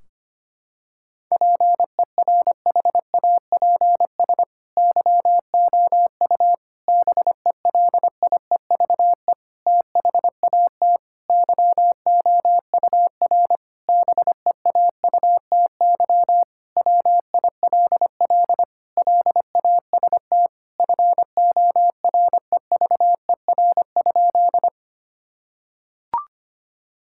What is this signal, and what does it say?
Morse code, heard as a keyed tone, about 25 wpm, 700 Hz. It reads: PERHAPS YOU BELIEVE THAT YOUR BEAUTY WILL LAST FOREVER?
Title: Sentences from Top 1600 Words 25wpm